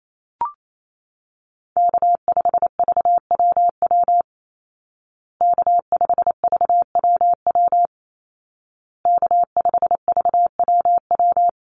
Morse code, heard as a keyed tone, about 28 wpm, 700 Hz. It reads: K5VWW K5VWW K5VWW
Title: Call Signs Repeated 2x - 28wpm